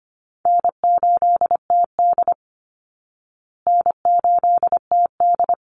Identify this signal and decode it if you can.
Morse code, heard as a keyed tone, about 25 wpm, 700 Hz. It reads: N8TD N8TD
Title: Easy Call Signs Repeated 2x - 25wpm